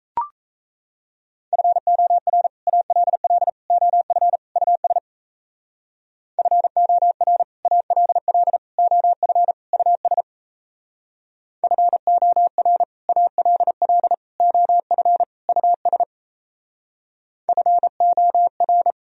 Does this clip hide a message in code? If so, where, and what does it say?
Yes: Morse code from 1.5 to 18.9 s, heard as a keyed tone, about 36 wpm, 700 Hz. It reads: FOR ALL OF US FOR ALL OF US FOR ALL OF US FOR